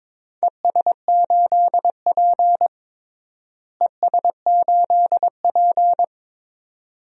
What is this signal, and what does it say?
Morse code, heard as a keyed tone, about 21 wpm, 700 Hz. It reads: ES8P ES8P
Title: DXCC-100 Call Signs - 4 Characters - Repeated 2x - 22wpm